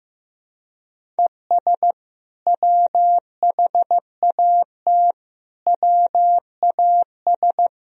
Morse code, heard as a keyed tone, about 15 wpm, 700 Hz. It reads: ES WHAT WAS
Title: Sentences from Top 100 Words and Abbreviations 15wpm